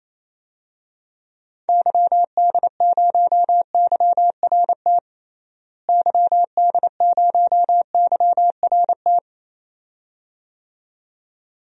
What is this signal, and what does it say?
Morse code, heard as a keyed tone, about 28 wpm, 700 Hz. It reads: YD0YRT YD0YRT
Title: DXCC-100 Call Signs - 6 Characters - Repeated 2x - 28wpm